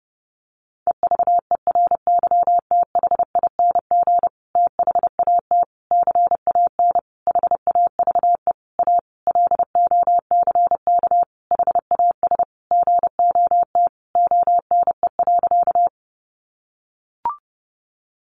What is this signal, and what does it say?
Morse code, heard as a keyed tone, about 30 wpm, 700 Hz. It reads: EVERYTHING THAT CAN HAVE A LOCK HAS GOT ONE.
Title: Sentences from Top 1300 Words 30wpm